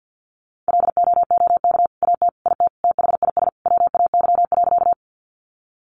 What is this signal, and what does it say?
Morse code, heard as a keyed tone, about 50 wpm, 700 Hz. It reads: LOOK AT IT THIS WAY.
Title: Sentences from Top 100 Words 50wpm